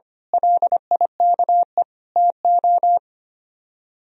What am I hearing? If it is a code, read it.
Morse code, heard as a keyed tone, about 25 wpm, 700 Hz. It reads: LIKE TO
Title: Speed-Racing - Sets of 4 Words 25wpm